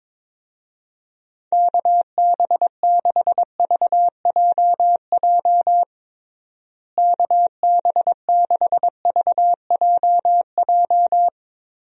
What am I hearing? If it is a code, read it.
Morse code, heard as a keyed tone, about 22 wpm, 700 Hz. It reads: KB6VJJ KB6VJJ
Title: USA Call Signs - 6 Characters - Repeated 3x - 22wpm